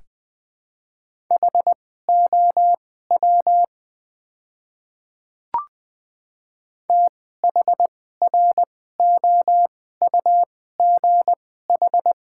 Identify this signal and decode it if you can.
Morse code, heard as a keyed tone, about 20 wpm, spaced as for 14 wpm, 700 Hz. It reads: HOW THROUGH
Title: 2x Character Spacing - Top 200 Words - 20wpm